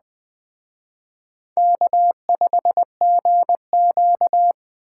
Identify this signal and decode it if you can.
Morse code, heard as a keyed tone, about 19 wpm, 700 Hz. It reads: K5GQ